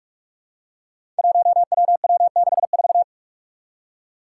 Morse code, heard as a keyed tone, about 44 wpm, 700 Hz. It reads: 1WW64
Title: Five Challenging Letter-Number Combinations 45wpm